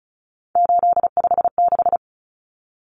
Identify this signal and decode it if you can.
Morse code, heard as a keyed tone, about 34 wpm, 700 Hz. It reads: 856
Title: Numbers - 3-Digit 35wpm